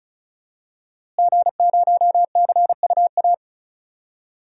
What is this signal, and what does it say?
Morse code, heard as a keyed tone, about 34 wpm, 700 Hz. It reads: G0CUA
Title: DXCC-100 Call Signs - 5 Characters - Repeated 3x - 35wpm